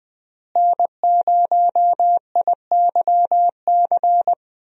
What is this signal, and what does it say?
Morse code, heard as a keyed tone, about 21 wpm, 700 Hz. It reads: N0IYC
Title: USA Call Signs - 5 Characters - Repeated 3x - 20wpm